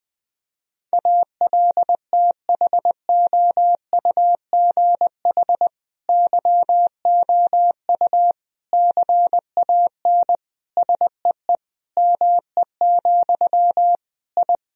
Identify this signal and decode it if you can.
Morse code, heard as a keyed tone, about 20 wpm, 700 Hz. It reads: ALTHOUGH YOU CAN SEE ME, I
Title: Sentences from Top 1000 Words - Review Part 3 - 20wpm